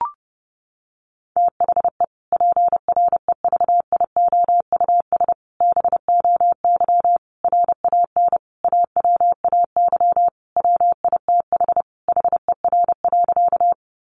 Morse code, heard as a keyed tone, about 30 wpm, 700 Hz. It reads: THE PREVIOUS BOY RAN AWAY WITH HER.
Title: Sentences from Top 1500 Words 30wpm